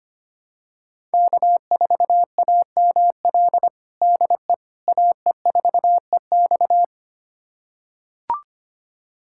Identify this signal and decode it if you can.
Morse code, heard as a keyed tone, about 25 wpm, 700 Hz. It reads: K4AML DE AE4EX
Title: USA Callsign DE Callsign - 5 Characters - 25wpm